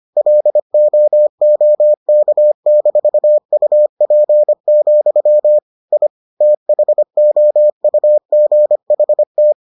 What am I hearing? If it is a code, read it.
Morse code, heard as a keyed tone, about 25 wpm, 600 Hz. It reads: LOOK-UP, I THOUGHT